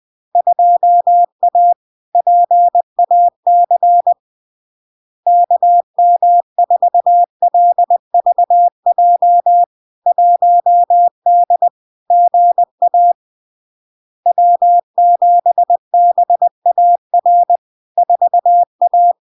Morse code, heard as a keyed tone, about 20 wpm, 700 Hz. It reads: 2A PAC KM4LVJ 1D GA W7BAR 4A
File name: LARK-FD-2022-Log-in-CW-20wpm-700-Hz